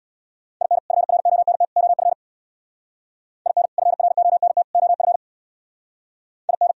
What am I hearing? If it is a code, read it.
Morse code, heard as a keyed tone, about 60 wpm, 700 Hz. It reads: IN FRONT OF IN FRONT OF IN
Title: Speed-Racing - Sets of 3 Words 50wpm